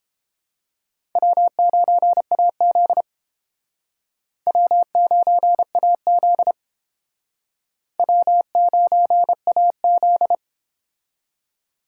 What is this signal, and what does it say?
Morse code, heard as a keyed tone, about 30 wpm, 700 Hz. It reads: W9AZ W9AZ W9AZ